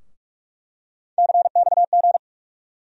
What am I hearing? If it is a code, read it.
Morse code, heard as a keyed tone, about 46 wpm, 700 Hz. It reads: CXG
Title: USA Call Sign Postfixes - 3 Letters 45wpm